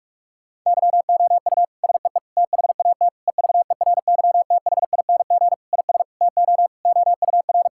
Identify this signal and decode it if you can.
Morse code, heard as a keyed tone, about 46 wpm, 700 Hz. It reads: YOU SEE THAT EVERYTHING IS TO OUR
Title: Sentences from Top 1000 Words - Review Part 6 - 45wpm